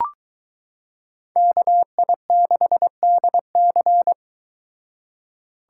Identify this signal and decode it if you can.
Morse code, heard as a keyed tone, about 23 wpm, 700 Hz. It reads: KI6DC